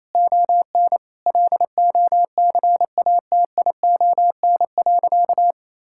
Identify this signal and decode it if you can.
Morse code, heard as a keyed tone, about 28 wpm, 700 Hz. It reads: ON LOCATION.
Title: Sentences from Top 1000 Words - Review Part 6 - 28wpm